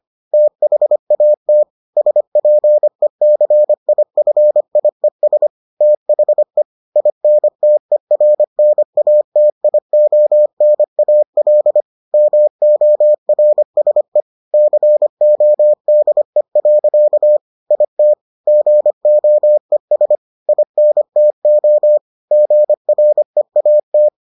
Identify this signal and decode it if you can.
Morse code, heard as a keyed tone, about 25 wpm, 600 Hz. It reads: THAT SPECIFIES THE INTERNATIONAL MORSE CODE. IT GOES INTO GREAT